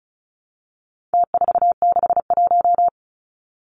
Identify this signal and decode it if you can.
Morse code, heard as a keyed tone, about 34 wpm, 700 Hz. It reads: T461
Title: Numbers - Contest Style - 4-Digit 35wpm